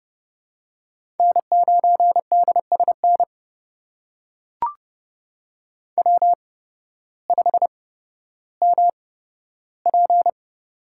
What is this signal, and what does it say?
Morse code, heard as a keyed tone, about 30 wpm, spaced as for 14 wpm, 700 Hz. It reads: N9DSN W5MP
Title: USA Call Signs - 8x Character Spacing - 30wpm